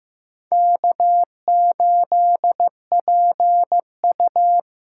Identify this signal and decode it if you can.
Morse code, heard as a keyed tone, about 15 wpm, 700 Hz. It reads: K8PU
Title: USA Call Signs - 4 Characters - 8x Character Spacing - 15wpm